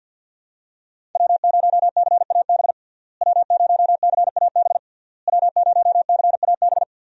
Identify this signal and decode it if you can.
Morse code, heard as a keyed tone, about 50 wpm, 700 Hz. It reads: W0CAB W0CAB W0CAB
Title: USA Call Signs - 5 Characters - Repeated 3x - 50wpm